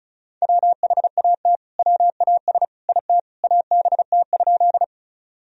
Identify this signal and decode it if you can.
Morse code, heard as a keyed tone, about 34 wpm, 700 Hz. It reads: WHAT WAS IT ABT?
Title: Sentences from Top 100 Words and Abbreviations 35wpm